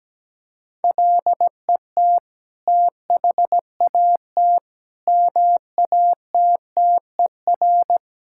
Morse code, heard as a keyed tone, about 17 wpm, 700 Hz. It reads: LET THAT MATTER
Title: Sentences from Top 1000 Words - Review Part 5 - 17wpm